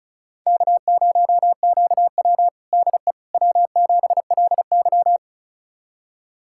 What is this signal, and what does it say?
Morse code, heard as a keyed tone, about 34 wpm, 700 Hz. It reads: K0QW DE W7LY
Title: USA Callsign DE Callsign - 4 Characters - 35wpm